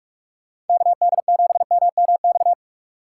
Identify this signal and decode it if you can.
Morse code, heard as a keyed tone, about 43 wpm, 700 Hz. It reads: KD7MMX
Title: USA Call Signs - 6 Characters - 3x Character Spacing - 45wpm